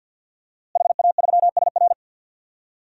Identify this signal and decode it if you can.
Morse code, heard as a keyed tone, about 51 wpm, 700 Hz. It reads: HA3SR
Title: DXCC-100 Call Signs - 5 Characters - Repeated 3x - 50wpm